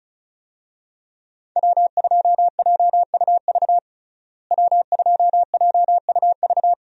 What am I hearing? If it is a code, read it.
Morse code, heard as a keyed tone, about 34 wpm, 700 Hz. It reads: W2JUV W2JUV
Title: USA Call Signs - 5 Characters - Repeated 2x - 35wpm